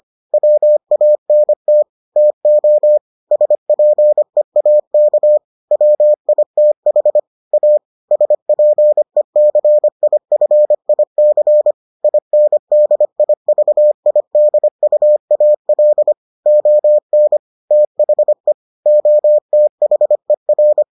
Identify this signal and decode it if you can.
Morse code, heard as a keyed tone, about 25 wpm, 600 Hz. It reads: WANT TO SPEAK WITH A SPECIFIC INDIVIDUAL ON THE OTHER